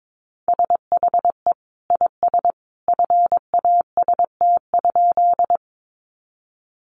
Morse code, heard as a keyed tone, about 22 wpm, 700 Hz. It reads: SHE IS FAST?
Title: Sentences from Top 400 Words 22wpm